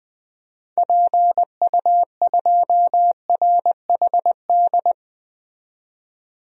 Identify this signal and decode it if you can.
Morse code, heard as a keyed tone, about 21 wpm, 700 Hz. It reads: PU2RHD
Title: DXCC-100 Call Signs - 6 Characters - 20wpm